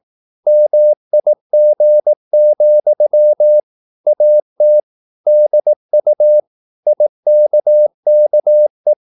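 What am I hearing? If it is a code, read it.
Morse code, heard as a keyed tone, about 18 wpm, 600 Hz. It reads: MIG, AT DU IKKE
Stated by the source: Slow (QRS) Morse practice at 18 wpm, CW med Gnister.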